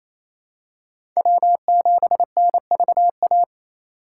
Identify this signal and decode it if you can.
Morse code, heard as a keyed tone, about 28 wpm, 700 Hz. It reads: W7NVA
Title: USA Call Signs - 5 Characters - 4x Character Spacing - 28wpm